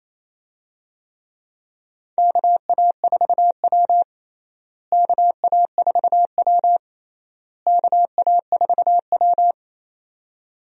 Morse code, heard as a keyed tone, about 28 wpm, 700 Hz. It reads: KA4W KA4W KA4W
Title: USA Call Signs - 4 Characters - Repeated 3x - 28wpm